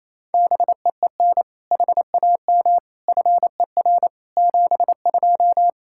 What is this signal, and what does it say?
Morse code, heard as a keyed tone, about 27 wpm, 700 Hz. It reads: BEEN HAM FER 72
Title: K1USN SST QSO Practice 28wpm